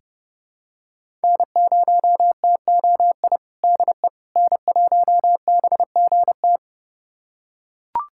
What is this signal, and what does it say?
Morse code, heard as a keyed tone, about 30 wpm, 700 Hz. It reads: N0TOI DE N1BGT
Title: USA Callsign DE Callsign - 5 Characters - 30wpm